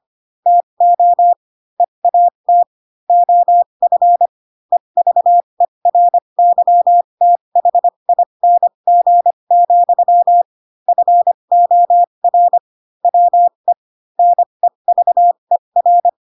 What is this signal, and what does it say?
Morse code, heard as a keyed tone, about 25 wpm, spaced as for 22 wpm, 700 Hz. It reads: TO EAT OF EVERYTHING, FOR WE NEVER
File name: Pinocchio / Pinocchio ch7 18 25 ik1hge